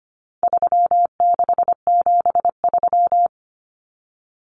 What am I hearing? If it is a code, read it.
Morse code, heard as a keyed tone, about 25 wpm, 700 Hz. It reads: 3673
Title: Common Cut Numbers - 4-Digit 25wpm